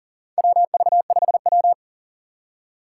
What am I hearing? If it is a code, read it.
Morse code, heard as a keyed tone, about 40 wpm, 700 Hz. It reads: WV5W
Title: USA Call Signs - 4 Characters - 3x Character Spacing - 40wpm